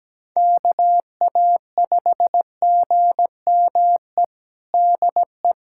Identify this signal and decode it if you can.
Morse code, heard as a keyed tone, about 17 wpm, 700 Hz. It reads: KA5GME DE
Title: USA Callsign DE Callsign - 6 Characters - 17wpm